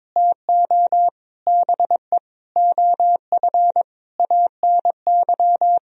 Morse code, heard as a keyed tone, about 22 wpm, 700 Hz. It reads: TO BE OF ANY
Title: Sentences from Top 1000 Words - Review Part 4 - 22wpm